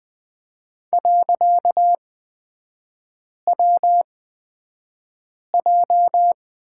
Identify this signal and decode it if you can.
Morse code, heard as a keyed tone, about 20 wpm, spaced as for 5 wpm, 700 Hz. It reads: .WJ
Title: 255 Lesson 41e - Period - Three Characters 20wpm